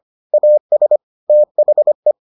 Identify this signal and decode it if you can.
Morse code, heard as a keyed tone, about 25 wpm, 600 Hz. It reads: AS THE